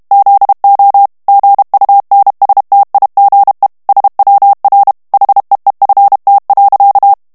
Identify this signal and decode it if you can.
Morse code, heard as a keyed tone, about 32 wpm, 800 Hz. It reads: ZO GUNSTIGE SWR HEEFT.
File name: morse mp3 / HOLLANDAIS texte clair 1 vitesse 30 mots minute